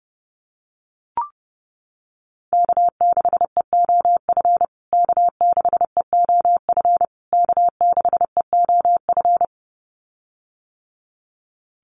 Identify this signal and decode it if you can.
Morse code, heard as a keyed tone, about 30 wpm, 700 Hz. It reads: K6EOF K6EOF K6EOF
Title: USA Call Signs Repeated 3x 30wpm